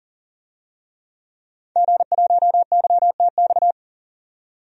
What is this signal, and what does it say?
Morse code, heard as a keyed tone, about 40 wpm, 700 Hz. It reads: G1YTX
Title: DXCC-100 Call Signs - 5 Characters - 40wpm